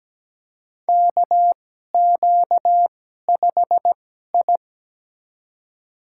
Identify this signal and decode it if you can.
Morse code, heard as a keyed tone, about 17 wpm, spaced as for 12 wpm, 700 Hz. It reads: KQ5I
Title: USA Call Signs - 4 Characters - 2x Character Spacing - 17wpm